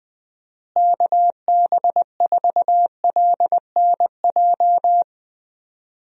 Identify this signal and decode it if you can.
Morse code, heard as a keyed tone, about 20 wpm, 700 Hz. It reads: KB4LNJ
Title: USA Call Signs - 6 Characters - Repeated 3x - 20wpm